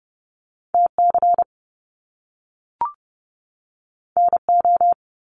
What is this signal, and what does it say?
Morse code, heard as a keyed tone, about 30 wpm, 700 Hz. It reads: TC NO
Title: Top Two Letter Combinations in Words 30wpm